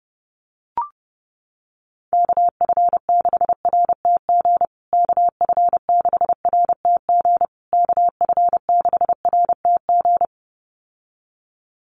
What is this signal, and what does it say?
Morse code, heard as a keyed tone, about 30 wpm, 700 Hz. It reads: KF6RTG KF6RTG KF6RTG
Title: USA Call Signs Repeated 3x 30wpm